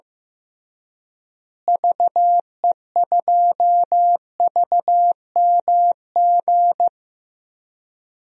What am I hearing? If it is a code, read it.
Morse code, heard as a keyed tone, about 15 wpm, 700 Hz. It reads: VE2VMG
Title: DXCC-100 Call Signs - 6 Characters - Repeated 3x - 15wpm